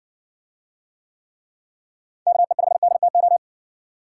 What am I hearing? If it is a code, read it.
Morse code, heard as a keyed tone, about 60 wpm, 700 Hz. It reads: KE5DTO